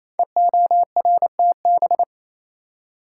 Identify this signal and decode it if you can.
Morse code, heard as a keyed tone, about 27 wpm, 700 Hz. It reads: EORTB
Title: Code Groups - 5 Letter:Numbers - 28wpm